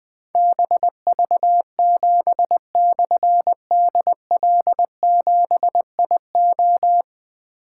Morse code, heard as a keyed tone, about 21 wpm, 700 Hz. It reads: BV7/DL7IO
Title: Difficult Call Signs 20wpm